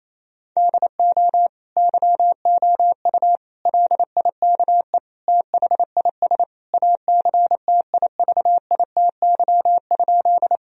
Morse code, heard as a keyed tone, about 28 wpm, 700 Hz. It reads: DO YOU LIKE THIS ACTIVITY?
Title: Sentences from Top 1000 Words - Review Part 4 - 28wpm